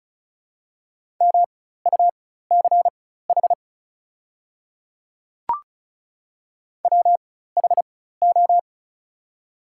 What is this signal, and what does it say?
Morse code, heard as a keyed tone, about 36 wpm, spaced as for 16 wpm, 700 Hz. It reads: MUCH WHO
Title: 4x Character Spacing - Top 300 Words - 35wpm